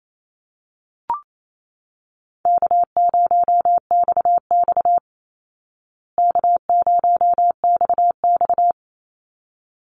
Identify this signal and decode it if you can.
Morse code, heard as a keyed tone, about 28 wpm, 700 Hz. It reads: K0XX K0XX
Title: Easy Call Signs Repeated 2x - 28wpm